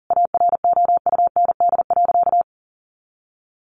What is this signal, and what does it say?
Morse code, heard as a keyed tone, about 40 wpm, 700 Hz. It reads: AROUND.
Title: Easy Phrases 40wpm